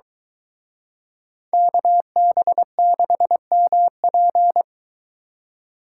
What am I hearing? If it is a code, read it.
Morse code, heard as a keyed tone, about 22 wpm, 700 Hz. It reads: KB6MP